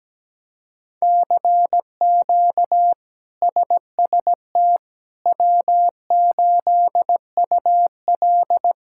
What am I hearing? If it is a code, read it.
Morse code, heard as a keyed tone, about 17 wpm, 700 Hz. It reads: CQ SST W8UL
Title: K1USN SST QSO Practice - 17wpm